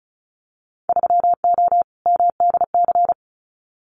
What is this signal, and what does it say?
Morse code, heard as a keyed tone, about 36 wpm, 700 Hz. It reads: 3O MDC
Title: Winter Field Day Exchanges Encore 35wpm